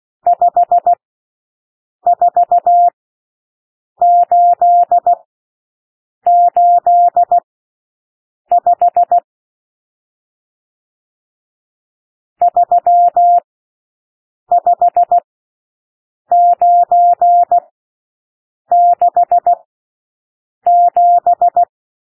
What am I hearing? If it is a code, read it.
Morse code, heard as a keyed tone, about 16 wpm, spaced as for 6 wpm, 700 Hz. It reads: 54885 35967
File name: SOUND NUMBERS 16 wpm